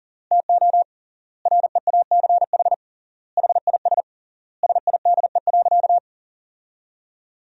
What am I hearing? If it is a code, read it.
Morse code, heard as a keyed tone, about 40 wpm, 700 Hz. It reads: TO REACH HIS SIDE.
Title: 3x Word-Spacing - Sentences from Top 400 Words - 40wpm